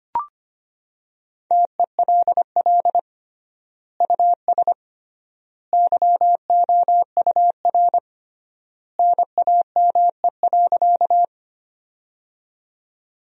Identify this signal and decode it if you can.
Morse code, heard as a keyed tone, about 25 wpm, 700 Hz. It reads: TELL US YOUR NAME.
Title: 3x Word-Spacing - Sentences from Top 200 Words - 25wpm